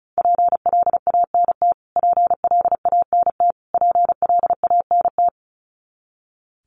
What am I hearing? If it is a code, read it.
Morse code, heard as a keyed tone, about 34 wpm, 700 Hz. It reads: PLANT PLANT PLANT
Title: Top 500 Words Repeated 35wpm